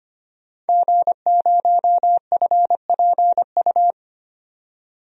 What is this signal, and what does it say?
Morse code, heard as a keyed tone, about 25 wpm, 700 Hz. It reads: G0FPU
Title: DXCC-100 Call Signs - 5 Characters - 25wpm